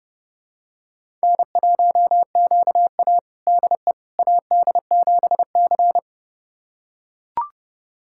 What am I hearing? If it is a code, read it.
Morse code, heard as a keyed tone, about 31 wpm, 700 Hz. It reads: N1QA DE AD7C
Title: USA Callsign DE Callsign - 4 Characters - 30wpm